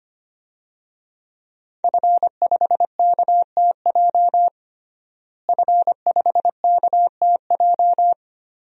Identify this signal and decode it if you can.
Morse code, heard as a keyed tone, about 25 wpm, 700 Hz. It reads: F5KTJ F5KTJ
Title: DXCC-100 Call Signs - 5 Characters - Repeated 3x - 25wpm